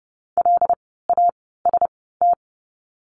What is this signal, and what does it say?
Morse code, heard as a keyed tone, about 30 wpm, spaced as for 17 wpm, 700 Hz. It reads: LAST
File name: course / 30:10wpm / 115 Lesson 19f - Letter w - Words 30wpm